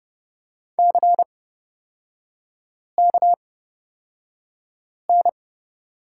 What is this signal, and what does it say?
Morse code, heard as a keyed tone, about 30 wpm, spaced as for 5 wpm, 700 Hz. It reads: CKN
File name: Lesson 37e - Letter x - Three Characters 30wpm